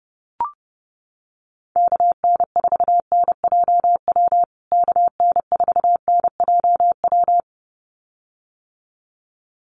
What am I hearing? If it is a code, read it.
Morse code, heard as a keyed tone, about 30 wpm, 700 Hz. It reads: KN4NJW KN4NJW
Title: USA Call Signs Repeated 2x 30wpm